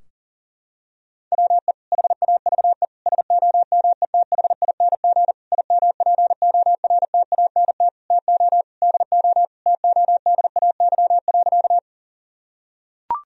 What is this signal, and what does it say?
Morse code, heard as a keyed tone, about 40 wpm, 700 Hz. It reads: WE HAVE SOMETHING IMPORTANT TO DO TODAY.
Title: Sentences from Top 1000 Words - Review Part 4 - 40wpm